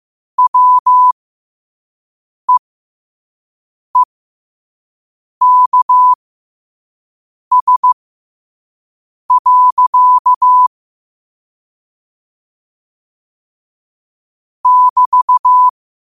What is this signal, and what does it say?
Morse code, heard as a keyed tone, about 15 wpm, spaced as for 5 wpm, 1000 Hz. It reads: WEEKS. <BT>
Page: New quotes every day in morse code at 5 Words per minute.